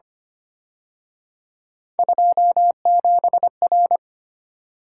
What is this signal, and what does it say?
Morse code, heard as a keyed tone, about 29 wpm, 700 Hz. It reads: 27R
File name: Speed-Racing - Three Letter-Number Combinations - 25wpm